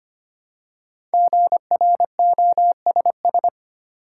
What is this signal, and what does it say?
Morse code, heard as a keyed tone, about 25 wpm, 700 Hz. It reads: GROSS
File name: 5 Letter Words - First 3 Letters Given - 25wpm